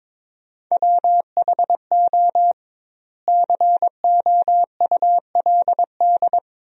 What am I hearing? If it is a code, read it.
Morse code, heard as a keyed tone, about 22 wpm, 700 Hz. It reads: WHO COULD
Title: 2x Word-Spacing - Sentences from Top 300 Words - 22wpm